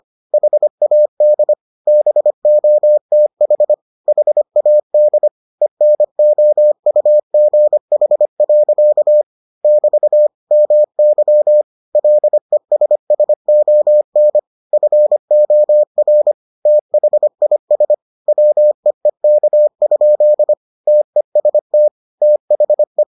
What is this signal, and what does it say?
Morse code, heard as a keyed tone, about 25 wpm, 600 Hz. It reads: HAD BOTH HAD ENOUGH. <BT> MY LESSON FOR THIS WEEK? TEST THE